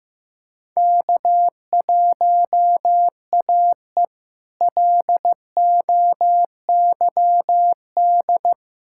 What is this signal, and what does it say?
Morse code, heard as a keyed tone, about 15 wpm, 700 Hz. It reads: K1AE LOYD
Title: K1USN SST QSO Practice - 15wpm